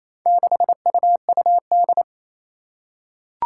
Morse code, heard as a keyed tone, about 28 wpm, 700 Hz. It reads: 6UUD
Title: Four Challenging Letter-Number Combinations 28wpm